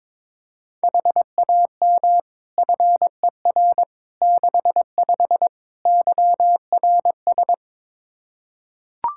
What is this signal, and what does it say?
Morse code, heard as a keyed tone, about 22 wpm, 700 Hz. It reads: HAM FER 65 YRS
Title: Practice QSO - Exchange 3 - More Info - 22wpm